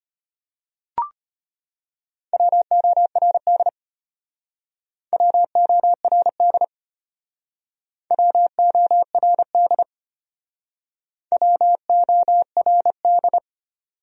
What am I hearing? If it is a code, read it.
Morse code, heard as a keyed tone, about 31 wpm, 700 Hz. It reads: WORD WORD WORD WORD
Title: Speed-Racing Top 100 Words 25wpm